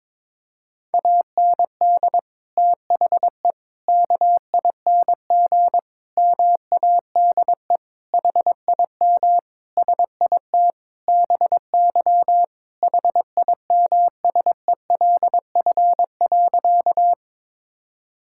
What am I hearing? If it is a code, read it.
Morse code, heard as a keyed tone, about 22 wpm, 700 Hz. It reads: AND THE KING MADE HIM SIT BY HIMSELF.
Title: Sentences from Top 1600 Words 22wpm